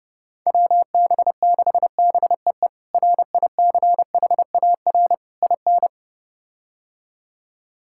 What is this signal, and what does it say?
Morse code, heard as a keyed tone, about 32 wpm, 700 Hz. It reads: WB6BEE RICHAR IN
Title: K1USN SST QSO Practice - 30wpm